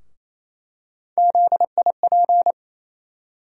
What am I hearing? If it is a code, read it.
Morse code, heard as a keyed tone, about 28 wpm, 700 Hz. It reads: ZIP
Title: USA Call Sign Postfixes - 3 Letters 28wpm